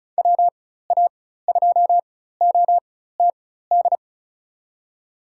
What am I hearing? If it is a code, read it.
Morse code, heard as a keyed tone, about 34 wpm, spaced as for 16 wpm, 700 Hz. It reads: WA2OTD
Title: USA Call Signs - 6 Characters - 4x Character Spacing - 35wpm